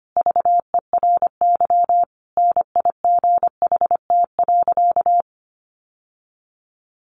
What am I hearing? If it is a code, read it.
Morse code, heard as a keyed tone, about 25 wpm, 700 Hz. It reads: VERY NIGHT.
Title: Sentences from Top 1200 Words 25wpm